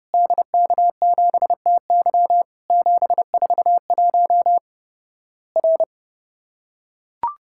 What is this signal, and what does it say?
Morse code, heard as a keyed tone, about 30 wpm, 700 Hz. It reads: DK7TY 741 R
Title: Worked All Europe DX Contest - QTC Exchange - 30wpm